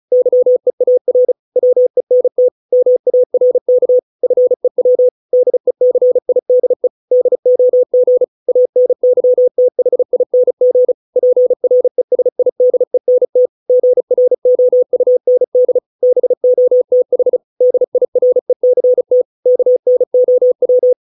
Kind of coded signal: Morse code